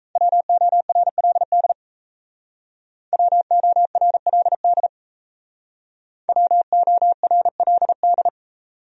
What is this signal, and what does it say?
Morse code, heard as a keyed tone, about 35 wpm, 700 Hz. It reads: WORLD WORLD WORLD
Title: Speed-Racing Top 200 Words 28wpm